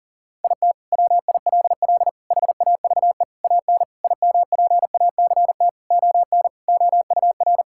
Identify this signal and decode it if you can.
Morse code, heard as a keyed tone, about 40 wpm, 700 Hz. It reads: IT WILL HAVE AN IMPACT ON OUR
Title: Sentences from Top 1000 Words - Review Part 6 - 40wpm